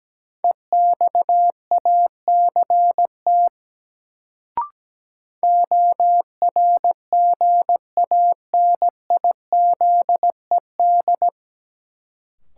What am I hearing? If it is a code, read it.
Morse code, heard as a keyed tone, about 17 wpm, 700 Hz. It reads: EXACT ORGANIZED
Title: Sentences from Top 1300 Words - New Word Drill Alt - 17wpm